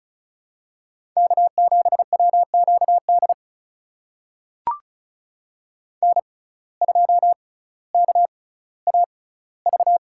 Morse code, heard as a keyed tone, about 36 wpm, spaced as for 18 wpm, 700 Hz. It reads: K7WQD N2KAV
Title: USA Call Signs - 5 Characters - 6x Character Spacing - 35wpm